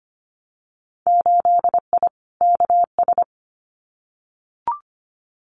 Morse code, heard as a keyed tone, about 25 wpm, 700 Hz. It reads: 8I KS
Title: Winter Field Day Exchanges Encore 25wpm